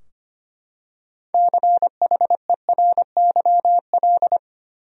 Morse code, heard as a keyed tone, about 25 wpm, 700 Hz. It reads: CHERYL
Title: USA Common Names - 25wpm